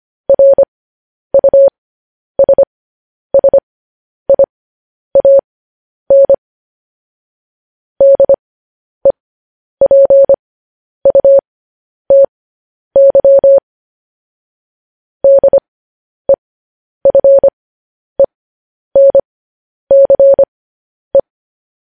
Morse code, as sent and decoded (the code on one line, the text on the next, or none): .-. ..- ... ... .. .- -. / -.. .--. ..- - -.-- / -.. ..-. -. -.-. .
RUSSIAN DEPUTY DEFENCE